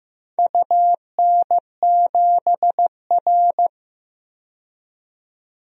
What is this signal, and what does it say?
Morse code, heard as a keyed tone, about 15 wpm, 700 Hz. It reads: UN7R
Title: DXCC-100 Call Signs - 4 Characters - 15wpm